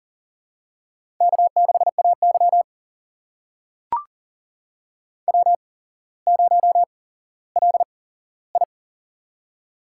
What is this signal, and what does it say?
Morse code, heard as a keyed tone, about 39 wpm, spaced as for 17 wpm, 700 Hz. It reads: K6AY W0LI
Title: USA Call Signs - 4 Characters - 8x Character Spacing - 40wpm